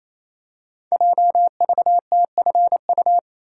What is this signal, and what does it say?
Morse code, heard as a keyed tone, about 27 wpm, 700 Hz. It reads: JVTFU
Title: Code Groups - 5 Letter:Numbers - 28wpm